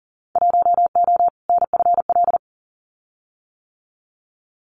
Morse code, heard as a keyed tone, about 40 wpm, 700 Hz. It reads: 1O NFL
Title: Winter Field Day Exchanges 40wpm